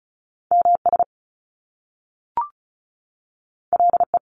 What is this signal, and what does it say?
Morse code, heard as a keyed tone, about 36 wpm, 700 Hz. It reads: MS LE
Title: Top Two Letter Combinations in Words 35wpm